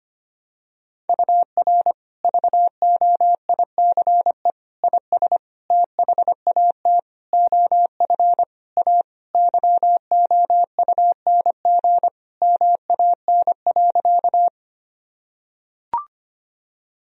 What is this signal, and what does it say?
Morse code, heard as a keyed tone, about 25 wpm, 700 Hz. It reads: UR VOICE IS THAT OF A YOUNG MAN.
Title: Sentences from Top 500 Words and Abbreviations 25wpm